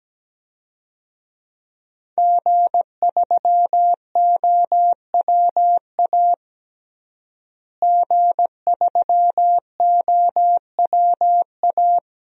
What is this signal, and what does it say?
Morse code, heard as a keyed tone, about 17 wpm, 700 Hz. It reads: G3OWA G3OWA
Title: DXCC-100 Call Signs - 5 Characters - Repeated 3x - 17wpm